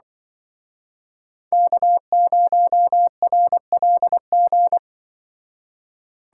Morse code, heard as a keyed tone, about 24 wpm, 700 Hz. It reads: K0RLG
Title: Speed Racing - USA 1x3 Call Signs 20wpm